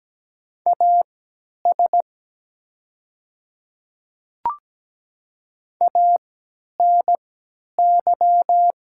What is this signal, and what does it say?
Morse code, heard as a keyed tone, about 17 wpm, spaced as for 10 wpm, 700 Hz. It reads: AS ANY
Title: 3x Character Spacing - Top 200 Words - 17wpm